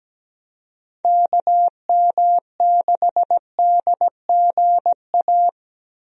Morse code, heard as a keyed tone, about 17 wpm, 700 Hz. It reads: KM6DGA
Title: USA Call Signs - 6 Characters - 17wpm